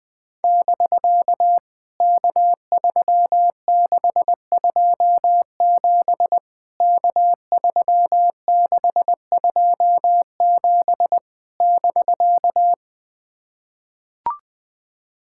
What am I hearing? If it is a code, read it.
Morse code, heard as a keyed tone, about 20 wpm, 700 Hz. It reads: <BK> K3627 K3627 <BK>
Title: POTA QSO Practice - 20wpm